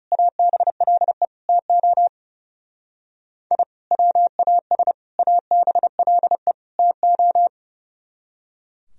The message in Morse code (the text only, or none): ABLE TO I WAS ABLE TO